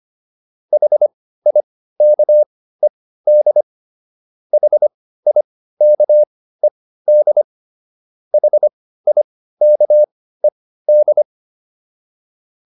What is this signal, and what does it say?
Morse code, heard as a keyed tone, about 25 wpm, spaced as for 15 wpm, 600 Hz. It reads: HIKED HIKED HIKED